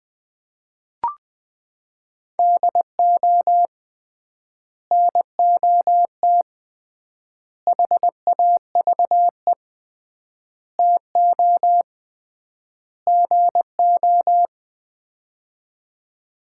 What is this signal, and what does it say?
Morse code, heard as a keyed tone, about 20 wpm, 700 Hz. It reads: DO NOT HAVE TO GO
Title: 3x Word-Spacing - Sets of 5 Words - 20wpm